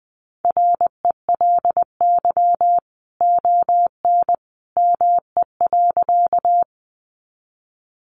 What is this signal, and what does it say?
Morse code, heard as a keyed tone, about 20 wpm, 700 Hz. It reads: RELY ON ME.
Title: Sentences from Top 1200 Words 20wpm